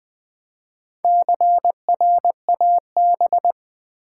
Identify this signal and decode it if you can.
Morse code, heard as a keyed tone, about 20 wpm, 700 Hz. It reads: CRAB
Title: Easy Words with Hints - 20wpm